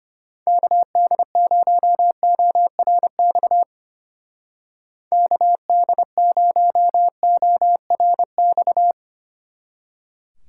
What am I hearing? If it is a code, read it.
Morse code, heard as a keyed tone, about 31 wpm, 700 Hz. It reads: KD0ORX KD0ORX
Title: Speed Racing - USA Call Signs 25wpm